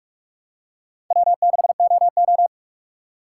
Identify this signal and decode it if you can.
Morse code, heard as a keyed tone, about 43 wpm, 700 Hz. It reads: W6OO
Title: USA Call Signs - 3x Character Spacing - 45wpm